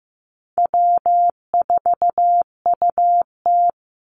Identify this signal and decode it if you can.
Morse code, heard as a keyed tone, about 15 wpm, 700 Hz. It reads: W4UT
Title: Easy Call Signs Repeated 2x - 15wpm